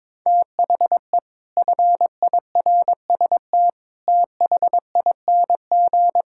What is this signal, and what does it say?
Morse code, heard as a keyed tone, about 21 wpm, 700 Hz. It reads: THE FIRST THING
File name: practice / 22wpm / Sets of 8 Words - 22wpm